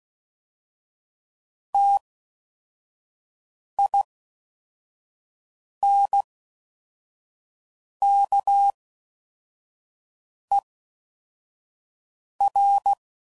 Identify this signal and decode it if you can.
Morse code, heard as a keyed tone, about 16 wpm, spaced as for 4 wpm, 800 Hz. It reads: TINKER